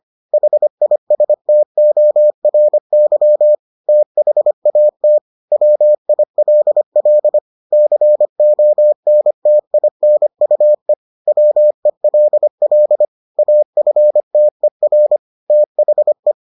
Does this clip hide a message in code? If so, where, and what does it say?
Yes: Morse code from 0.3 to 16.3 s, heard as a keyed tone, about 25 wpm, 600 Hz. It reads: HISTORY THAT WILL CONTINUE WELL AFTER THE